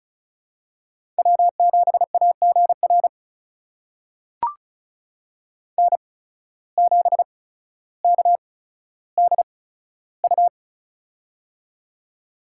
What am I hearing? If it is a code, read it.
Morse code, heard as a keyed tone, about 36 wpm, spaced as for 15 wpm, 700 Hz. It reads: W7AGR N7KDU
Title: USA Call Signs - 5 Characters - 8x Character Spacing - 35wpm